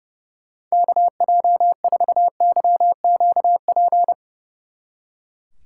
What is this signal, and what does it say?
Morse code, heard as a keyed tone, about 37 wpm, 700 Hz. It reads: KJ4YQP
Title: Speed Racing - USA 2x3 Call Signs 30wpm